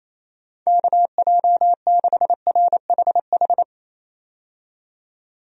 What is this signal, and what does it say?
Morse code, heard as a keyed tone, about 28 wpm, 700 Hz. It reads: KJ6RHH
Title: USA Call Signs - 6 Characters - 28wpm